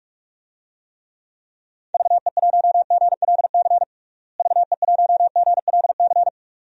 Morse code, heard as a keyed tone, about 41 wpm, 700 Hz. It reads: VE1GLC VE1GLC
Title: DXCC-100 Call Signs - 6 Characters - Repeated 2x - 45wpm